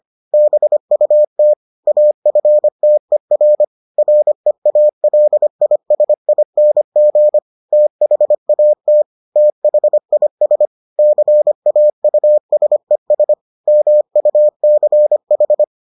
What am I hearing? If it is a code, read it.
Morse code, heard as a keyed tone, about 25 wpm, 600 Hz. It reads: BUT AFTER REALISING THAT THIS CAUSES MUCH